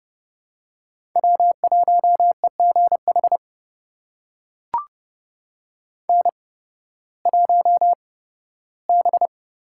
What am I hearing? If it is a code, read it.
Morse code, heard as a keyed tone, about 30 wpm, spaced as for 16 wpm, 700 Hz. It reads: W1EGH N1B
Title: USA Call Signs - 5 Characters - 8x Character Spacing - 30wpm